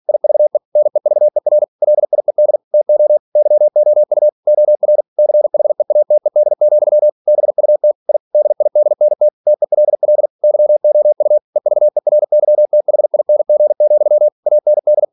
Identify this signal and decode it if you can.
Morse code, heard as a keyed tone, about 46 wpm, 600 Hz. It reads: IVE NEVER LIED TO YOU OR CHEATED, BUT I DIDNT TELL YOU EVERYTHING, AND
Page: Never 47wpm